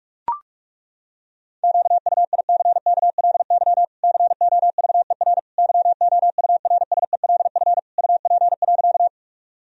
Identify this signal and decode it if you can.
Morse code, heard as a keyed tone, about 42 wpm, 700 Hz. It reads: QUICKLY COVER YOURSELF UP.
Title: Sentences from Top 1000 Words - Review Part 1 - 45wpm